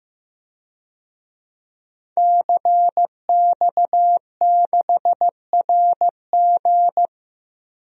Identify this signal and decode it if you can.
Morse code, heard as a keyed tone, about 15 wpm, 700 Hz. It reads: CX6RG
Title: DXCC-100 Call Signs - 5 Characters - Repeated 2x - 15wpm